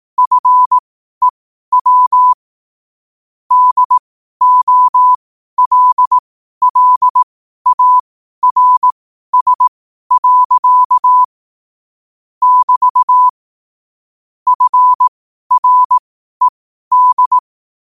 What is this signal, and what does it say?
Morse code, heard as a keyed tone, about 18 wpm, spaced as for 12 wpm, 1000 Hz. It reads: FEW DOLLARS. <BT> FRED
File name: qotdfiles / 12wpm